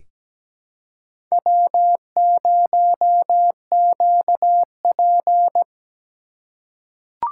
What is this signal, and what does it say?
Morse code, heard as a keyed tone, about 17 wpm, 700 Hz. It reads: W0QP
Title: USA Call Signs - 4 Characters - 2x Character Spacing - 17wpm